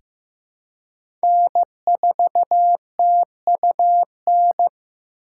Morse code, heard as a keyed tone, about 15 wpm, 700 Hz. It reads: N4TUN
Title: USA Call Signs - 5 Characters - 2x Character Spacing - 15wpm